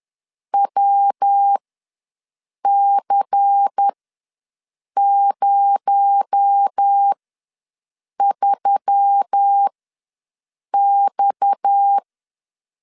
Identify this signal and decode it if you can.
Morse code, heard as a keyed tone, about 10 wpm, spaced as for 6 wpm, 800 Hz. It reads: WC03X
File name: LEZIONI / 6 PKX0 / 6 PKX0 ws 10wpm 4220